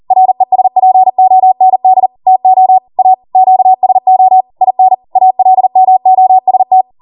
Morse code, heard as a keyed tone, about 40 wpm, 750 Hz. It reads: RESPOND TO A QSO IN ALMOST